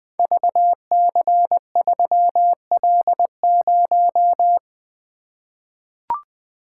Morse code, heard as a keyed tone, about 20 wpm, 700 Hz. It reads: VC3L0
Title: Code Groups - 5 Letter:Numbers - 20wpm